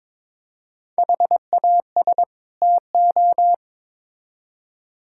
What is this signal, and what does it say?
Morse code, heard as a keyed tone, about 22 wpm, 700 Hz. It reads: HAS TO
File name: Build A Phrase - 2-Word Phrases - 22wpm